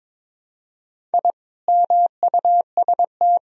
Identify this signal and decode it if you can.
Morse code, heard as a keyed tone, about 22 wpm, 700 Hz. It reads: I MUST
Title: Sentences from Top 1000 Words - Review Part 4 - 22wpm